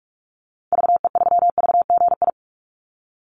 Morse code, heard as a keyed tone, about 44 wpm, 700 Hz. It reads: VE3VGI
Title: Call Signs Repeated 3x - 45wpm